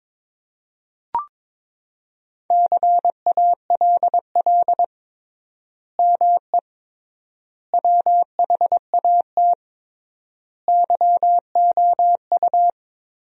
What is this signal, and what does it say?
Morse code, heard as a keyed tone, about 22 wpm, 700 Hz. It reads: CALL ME WHAT YOU
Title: 3x Word-Spacing - Sentences from Top 200 Words - 22wpm